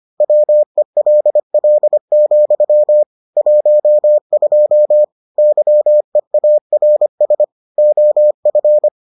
Morse code, heard as a keyed tone, about 25 wpm, 600 Hz. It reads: WELL, 12 YEARS OF